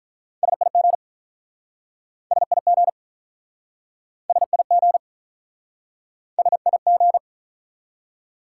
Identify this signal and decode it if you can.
Morse code, heard as a keyed tone, about 47 wpm, 700 Hz. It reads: SIG SIG SIG SIG